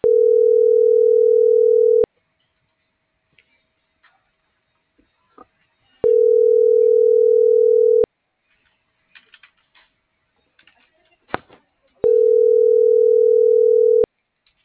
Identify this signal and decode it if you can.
Ringback tone (North America), 440 + 480 Hz.